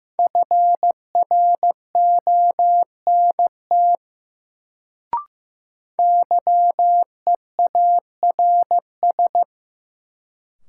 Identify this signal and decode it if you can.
Morse code, heard as a keyed tone, about 15 wpm, 700 Hz. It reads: FRONT YEARS
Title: Sets of 2 Words - New Word Drill - 15wpm